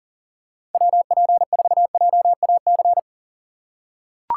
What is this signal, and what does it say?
Morse code, heard as a keyed tone, about 40 wpm, 700 Hz. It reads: WP4JAC
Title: USA Call Signs - 6 Characters - 6x Character Spacing - 40wpm